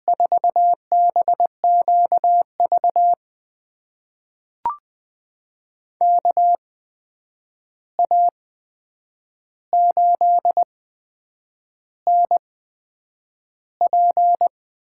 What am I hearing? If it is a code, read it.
Morse code, heard as a keyed tone, about 20 wpm, spaced as for 8 wpm, 700 Hz. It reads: 4BQV KA8NP
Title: USA Call Signs - 6 Characters - 8x Character Spacing - 20wpm